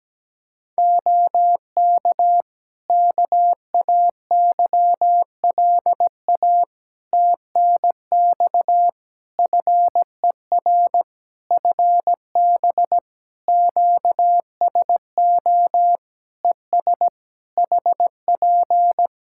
Morse code, heard as a keyed tone, about 17 wpm, 700 Hz. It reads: OK KAYLA TNX FER FB QSO ES HP
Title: K1USN SST QSO Practice 17wpm